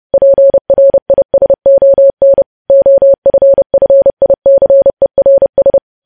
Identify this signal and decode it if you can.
Morse code, heard as a keyed tone, about 30 wpm, 550 Hz. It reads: PRISON OFFICERS
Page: news headlines in morse code at 30 wpm